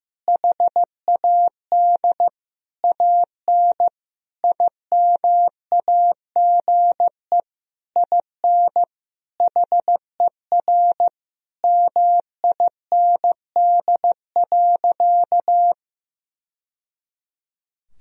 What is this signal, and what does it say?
Morse code, heard as a keyed tone, about 15 wpm, 700 Hz. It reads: HAD AN IMAGE IN HER MIND.
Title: Sentences from Top 1000 Words - Review Part 4 - 15wpm